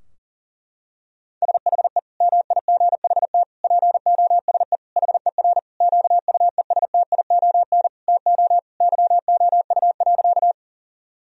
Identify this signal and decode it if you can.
Morse code, heard as a keyed tone, about 40 wpm, 700 Hz. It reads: SHE MIGHT POSE HER QUESTION TO YOU.
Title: Sentences from Top 1000 Words - Review Part 2 - 40wpm